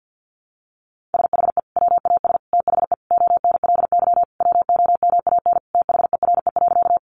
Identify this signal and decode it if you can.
Morse code, heard as a keyed tone, about 49 wpm, 700 Hz. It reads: SHE WAS THE ONLY WOMAN THERE.
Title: Sentences from Top 1100 Words 50wpm